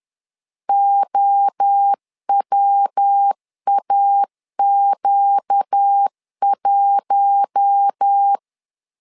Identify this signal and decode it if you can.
Morse code, heard as a keyed tone, about 10 wpm, 800 Hz. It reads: OWAQ1
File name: LEZIONI / 4 Q1 10wpm 2448